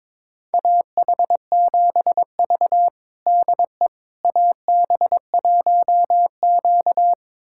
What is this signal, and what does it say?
Morse code, heard as a keyed tone, about 22 wpm, 700 Hz. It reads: AH7V DE AB1Q
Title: USA Callsign DE Callsign - 4 Characters - 22wpm